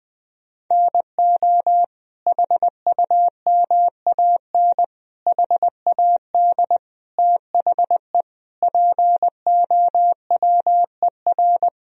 Morse code, heard as a keyed tone, about 19 wpm, 700 Hz. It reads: NO HUMAN HAD THE POWER
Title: Sentences from Top 1000 Words - Review Part 3 - 20wpm